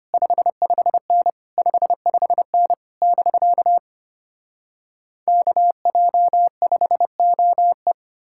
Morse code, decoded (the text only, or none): 55N 55N <BK> KJ5OE